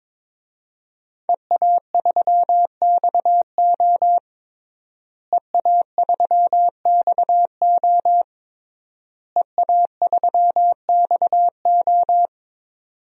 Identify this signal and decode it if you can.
Morse code, heard as a keyed tone, about 21 wpm, 700 Hz. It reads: EA3XO EA3XO EA3XO
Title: DXCC-100 Call Signs - 5 Characters - Repeated 3x - 22wpm